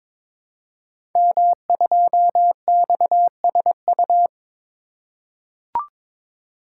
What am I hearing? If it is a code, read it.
Morse code, heard as a keyed tone, about 22 wpm, 700 Hz. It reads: M2XSU
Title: Code Groups - 5 Letter:Numbers - 22wpm